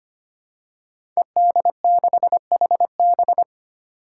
Morse code, heard as a keyed tone, about 25 wpm, 700 Hz. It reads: ED6HB
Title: Five Challenging Letter-Number Combinations 25wpm